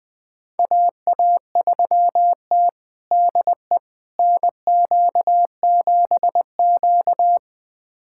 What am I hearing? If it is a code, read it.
Morse code, heard as a keyed tone, about 21 wpm, 700 Hz. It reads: AA3T DE NQ7Q
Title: USA Callsign DE Callsign - 4 Characters - 20wpm